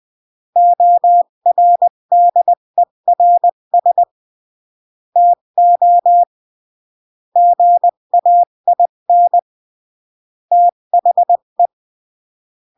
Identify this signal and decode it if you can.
Morse code, heard as a keyed tone, about 20 wpm, 700 Hz. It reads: ORDERS TO GAIN THE
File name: CW / jodorowsky / 82 commandments 20-15